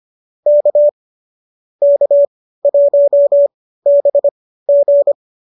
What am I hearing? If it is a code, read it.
Morse code, heard as a keyed tone, about 25 wpm, spaced as for 15 wpm, 570 Hz. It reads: K K1BG